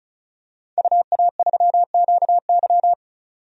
Morse code, heard as a keyed tone, about 36 wpm, 700 Hz. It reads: UA3QY
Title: DXCC-100 Call Signs - 5 Characters - 35wpm